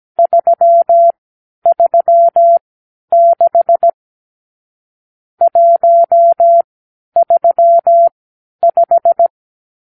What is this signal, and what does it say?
Morse code, heard as a keyed tone, about 17 wpm, spaced as for 10 wpm, 680 Hz. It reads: 336 135
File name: k7qo / Lesson 52 (8)